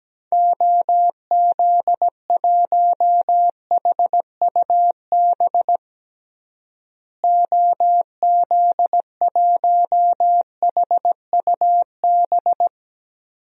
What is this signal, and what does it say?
Morse code, heard as a keyed tone, about 17 wpm, 700 Hz. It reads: OZ1HUB OZ1HUB
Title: DXCC-100 Call Signs - 6 Characters - Repeated 2x - 17wpm